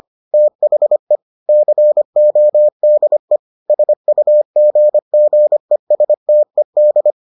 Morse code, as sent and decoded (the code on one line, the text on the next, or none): - .... / -.-. --- -.. / ... ..- --. --. ... - . -..
THE CODE SUGGESTED